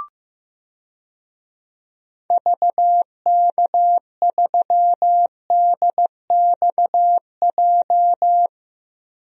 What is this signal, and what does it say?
Morse code, heard as a keyed tone, about 15 wpm, 700 Hz. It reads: VK3DXJ
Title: DXCC-100 Call Signs - 6 Characters - 15wpm